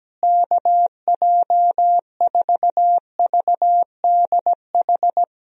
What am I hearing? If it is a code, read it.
Morse code, heard as a keyed tone, about 17 wpm, 700 Hz. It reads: KJ4VDH
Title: USA Call Signs - 6 Characters - Repeated 2x - 17wpm